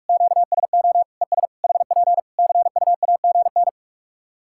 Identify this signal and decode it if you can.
Morse code, heard as a keyed tone, about 44 wpm, 700 Hz. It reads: QSO ES HP CUAGN
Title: K1USN SST QSO Practice 45wpm